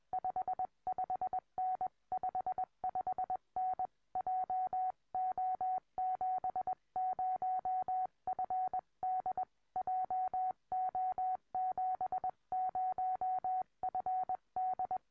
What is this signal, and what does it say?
Morse code, heard as a keyed tone, about 21 wpm, 740 Hz. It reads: 55N55N JO70FD JO70FD
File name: tropo / sm4hfi